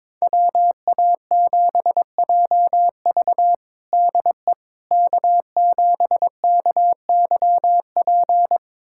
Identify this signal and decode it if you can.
Morse code, heard as a keyed tone, about 22 wpm, 700 Hz. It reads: WA7JV DE K7KYP
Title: USA Callsign DE Callsign - 5 Characters - 22wpm